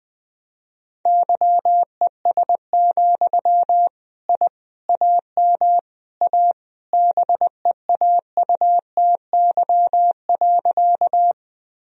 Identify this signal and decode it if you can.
Morse code, heard as a keyed tone, about 20 wpm, 700 Hz. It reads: YES, I AM A BEAUTY.
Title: Sentences from Top 1000 Words - Review Part 2 - 20wpm